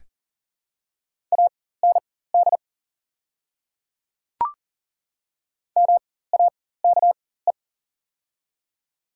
Morse code, heard as a keyed tone, about 38 wpm, spaced as for 18 wpm, 700 Hz. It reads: AND MAKE
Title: 4x Character Spacing - Top 100 Words - 40wpm